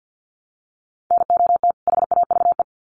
Morse code, heard as a keyed tone, about 50 wpm, 700 Hz. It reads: NOT HAVE
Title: Sets of 2 Words 50wpm